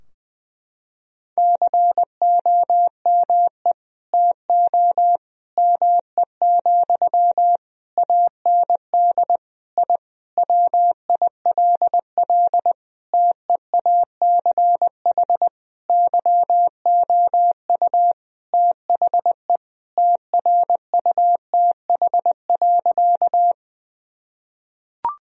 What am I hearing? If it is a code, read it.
Morse code, heard as a keyed tone, about 20 wpm, 700 Hz. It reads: COME TO ME, AND I WILL TEACH YOU THE TRUTH.
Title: Sentences from Top 1000 Words - Review Part 2 - 20wpm